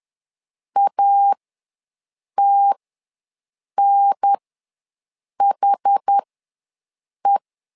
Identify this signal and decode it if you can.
Morse code, heard as a keyed tone, about 10 wpm, spaced as for 6 wpm, 800 Hz. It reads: ATNHE
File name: 2 UHDN ws 10wpm